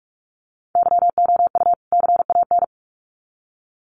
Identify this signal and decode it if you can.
Morse code, heard as a keyed tone, about 44 wpm, 700 Hz. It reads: YOU CAN
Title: Sets of 2 Words 45wpm